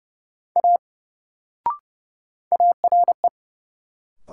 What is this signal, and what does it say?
Morse code, heard as a keyed tone, about 30 wpm, 700 Hz. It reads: A ARE
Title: Sentences from Top 100 Words - New Word Drill - 30wpm